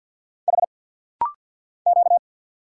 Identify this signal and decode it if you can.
Morse code, heard as a keyed tone, about 48 wpm, 700 Hz. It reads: H Q
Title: Single Letter-Number - Rapid-Fire 50wpm